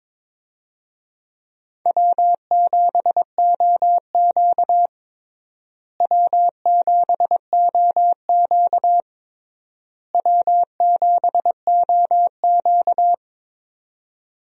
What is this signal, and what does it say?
Morse code, heard as a keyed tone, about 22 wpm, 700 Hz. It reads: W7OQ W7OQ W7OQ
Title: USA Call Signs - 4 Characters - Repeated 3x - 22wpm